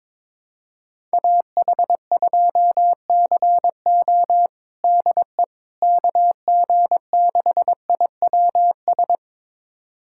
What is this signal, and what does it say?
Morse code, heard as a keyed tone, about 21 wpm, 700 Hz. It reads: AH2CO DE KG6IWS
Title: USA Callsign DE Callsign 22wpm